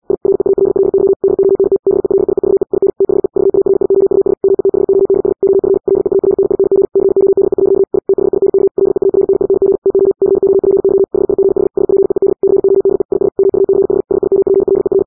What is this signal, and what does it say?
Morse code, heard as a keyed tone, about 80 wpm, 390 Hz. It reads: I REALLY DONT BELIEVE IN THE VARIOUS METHODS OF LEARNING CODE. I THINK LEARNING TO COPY HIGH SPEED CODE IS MUCH SIMPLER